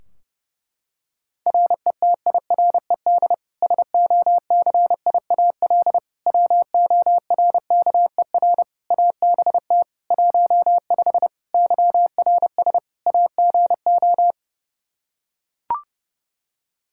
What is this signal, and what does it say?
Morse code, heard as a keyed tone, about 30 wpm, 700 Hz. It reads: RETIRED SOCIAL WORKER ABT 15 YRS AGO
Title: K1USN SST QSO Practice 30wpm